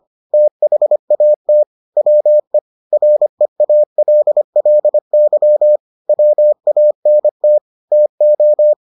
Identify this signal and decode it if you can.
Morse code, heard as a keyed tone, about 25 wpm, 600 Hz. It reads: THAT WE REALLY WANT TO